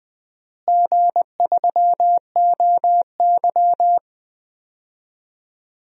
Morse code, heard as a keyed tone, about 20 wpm, 700 Hz. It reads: G3OY
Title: DXCC-100 Call Signs - 4 Characters - Repeated 3x - 20wpm